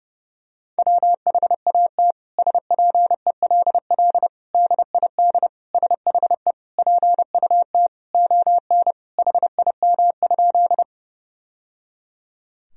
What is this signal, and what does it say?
Morse code, heard as a keyed tone, about 32 wpm, 700 Hz. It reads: WHAT SPELL DID SHE PUT ON HIM?
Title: Sentences from Top 1000 Words - Review Part 1 - 30wpm